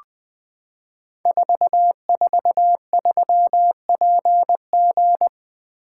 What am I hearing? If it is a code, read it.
Morse code, heard as a keyed tone, about 20 wpm, 700 Hz. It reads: 443PG
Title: Code Groups - 5 Letter:Numbers - 20wpm